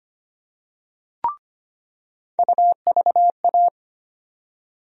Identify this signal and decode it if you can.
Morse code, heard as a keyed tone, about 25 wpm, 700 Hz. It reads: UVA
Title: USA Call Sign Postfixes - 3 Letters 25wpm